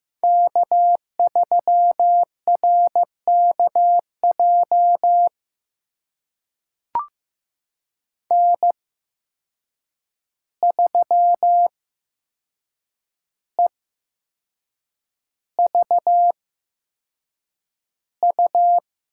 Morse code, heard as a keyed tone, about 15 wpm, spaced as for 6 wpm, 700 Hz. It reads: K3RKJ N3EVU
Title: USA Call Signs - 5 Characters - 8x Character Spacing - 15wpm